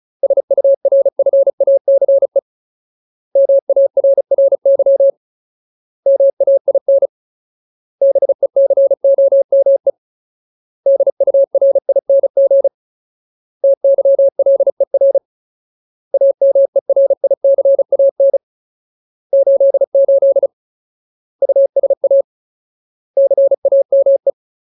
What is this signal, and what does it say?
Morse code, heard as a keyed tone, about 34 wpm, 550 Hz. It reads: SURFACE MARRY MAIN BECOME DURING TYLER AMERICAN 88 USA CAME
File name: Ultimate Word List / 35WPM Ultimate